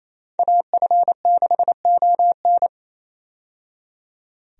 Morse code, heard as a keyed tone, about 28 wpm, 700 Hz. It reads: AF6ON
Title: USA Call Signs - 5 Characters - 28wpm